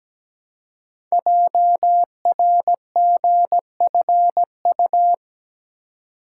Morse code, heard as a keyed tone, about 17 wpm, 700 Hz. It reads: JRGFU
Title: Code Groups - 5 Letters - 17wpm